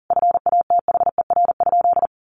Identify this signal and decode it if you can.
Morse code, heard as a keyed tone, about 40 wpm, 700 Hz. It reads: FATHER?
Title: Sentences from Top 1400 Words 40wpm